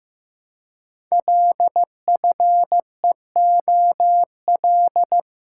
Morse code, heard as a keyed tone, about 15 wpm, 700 Hz. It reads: LFEOL
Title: Code Groups - 5 Letters - 15wpm